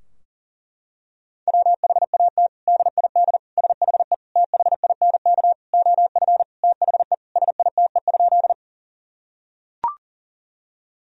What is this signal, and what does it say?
Morse code, heard as a keyed tone, about 40 wpm, 700 Hz. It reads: WHAT DID SHE THINK OF THE SITE?
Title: Sentences from Top 1000 Words - Review Part 4 - 40wpm